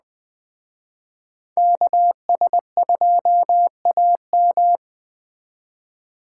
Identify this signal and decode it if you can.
Morse code, heard as a keyed tone, about 20 wpm, 700 Hz. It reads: KS2AM